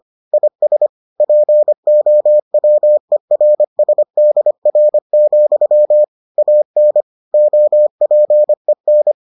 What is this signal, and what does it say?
Morse code, heard as a keyed tone, about 25 wpm, 600 Hz. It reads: IS POWERSDR, AN OPEN